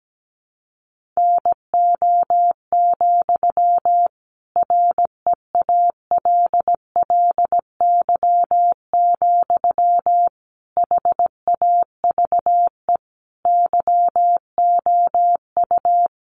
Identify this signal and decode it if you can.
Morse code, heard as a keyed tone, about 17 wpm, 700 Hz. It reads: NO, REALLY, HAVE YOU
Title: Sentences from Top 1000 Words 17wpm